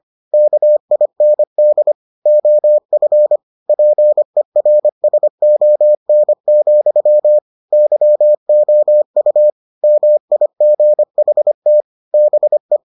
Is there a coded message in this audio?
Yes: Morse code from 0.3 to 12.8 s, heard as a keyed tone, about 25 wpm, 600 Hz. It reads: KIND OF PERSON, YOU MIGHT BE